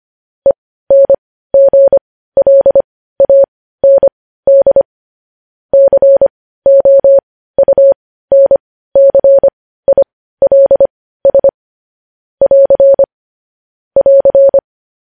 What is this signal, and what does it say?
Morse code, heard as a keyed tone, about 25 wpm, spaced as for 15 wpm, 550 Hz. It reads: ENGLAND COUNCILS <AR> <AR>
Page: news-headlines-in-morse-code-at-15-wpm